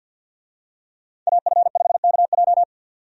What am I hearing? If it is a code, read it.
Morse code, heard as a keyed tone, about 47 wpm, 700 Hz. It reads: AF5KJ
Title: USA Call Signs - 5 Characters - 50wpm